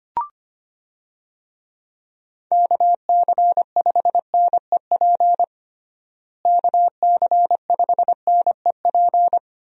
Morse code, heard as a keyed tone, about 25 wpm, 700 Hz. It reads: KC5NEP KC5NEP
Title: USA Call Signs - 6 Characters - Repeated 2x - 25wpm